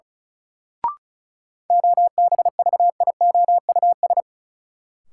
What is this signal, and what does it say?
Morse code, heard as a keyed tone, about 36 wpm, 700 Hz. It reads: OBVIOUS
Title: Sentences from Top 1600 Words - New Word Drill - 35wpm